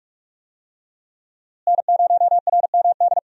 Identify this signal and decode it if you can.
Morse code, heard as a keyed tone, about 46 wpm, 700 Hz. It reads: N0RMD
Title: USA Call Signs - 5 Characters - Repeated 3x - 45wpm